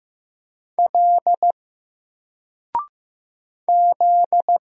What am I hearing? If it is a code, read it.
Morse code, heard as a keyed tone, about 15 wpm, 700 Hz. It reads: L Z
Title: Single Letter-Number 15wpm